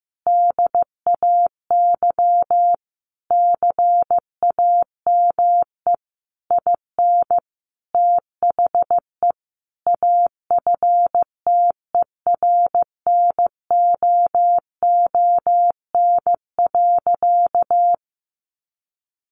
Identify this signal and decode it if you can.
Morse code, heard as a keyed tone, about 15 wpm, 700 Hz. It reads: DAY CAME IN THE AFTERNOON.
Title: Sentences from Top 1500 Words 15wpm